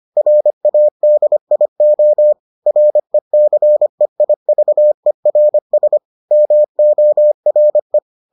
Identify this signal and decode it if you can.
Morse code, heard as a keyed tone, about 25 wpm, 600 Hz. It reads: RADIO RECEIVERS MORE